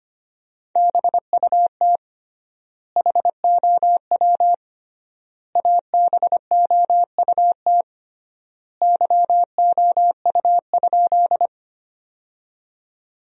Morse code, heard as a keyed tone, about 25 wpm, 700 Hz. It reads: BUT HOW ABOUT YOU?
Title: 3x Word-Spacing - Sentences from Top 100 Words - 25wpm